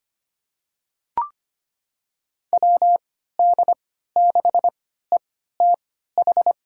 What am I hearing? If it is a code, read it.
Morse code, heard as a keyed tone, about 25 wpm, spaced as for 14 wpm, 690 Hz. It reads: WD6ETH
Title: USA Call Signs - 6 Characters - 3x Character Spacing - 25wpm